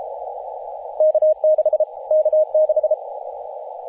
Morse code, heard as a keyed tone, about 34 wpm, 620 Hz, with radio noise. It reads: K6 K6
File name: t30cow-18-072mhz-cw